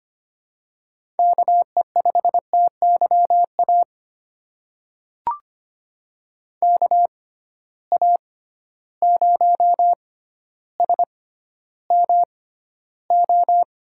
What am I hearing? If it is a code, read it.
Morse code, heard as a keyed tone, about 25 wpm, spaced as for 13 wpm, 700 Hz. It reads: KE5TYA KA0SMO
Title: USA Call Signs - 6 Characters - 6x Character Spacing - 25wpm